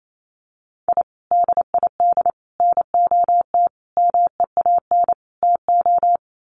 Morse code, heard as a keyed tone, about 28 wpm, 700 Hz. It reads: I DID NOT MEAN TO
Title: Sets of 5 Words 28wpm